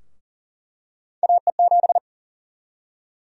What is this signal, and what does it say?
Morse code, heard as a keyed tone, about 40 wpm, 700 Hz. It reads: AE7
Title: USA Call Sign Prefixes - 40wpm